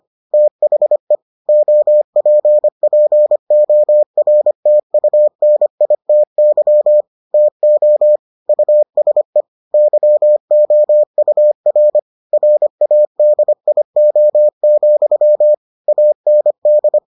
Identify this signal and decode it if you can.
Morse code, heard as a keyed tone, about 25 wpm, 600 Hz. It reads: THE OPPORTUNITY TO USE YOUR RADIO, AND